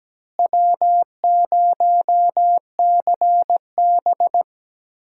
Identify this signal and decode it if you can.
Morse code, heard as a keyed tone, about 17 wpm, 700 Hz. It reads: W0CB
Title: USA Call Signs - 4 Characters - Repeated 3x - 17wpm